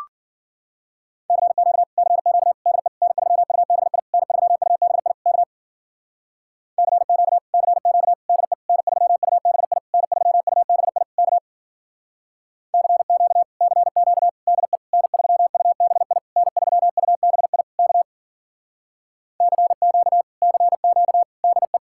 Morse code, heard as a keyed tone, about 52 wpm, 700 Hz. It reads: CQ CQ DE N3UBI N3UBI K CQ CQ DE N3UBI N3UBI K CQ CQ DE N3UBI N3UBI K CQ CQ DE